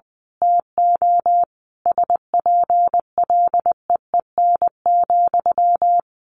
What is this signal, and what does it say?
Morse code, heard as a keyed tone, about 20 wpm, 700 Hz. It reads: TO SPLEEN,
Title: A Sound Of Thunder 20wpm